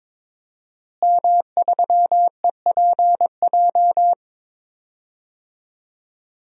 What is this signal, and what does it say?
Morse code, heard as a keyed tone, about 22 wpm, 700 Hz. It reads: M3EPJ
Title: DXCC-100 Call Signs - 5 Characters - Repeated 2x - 22wpm